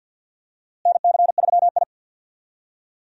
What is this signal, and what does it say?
Morse code, heard as a keyed tone, about 46 wpm, 700 Hz. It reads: NC3I
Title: USA Call Signs - 4 Characters - Repeated 2x - 50wpm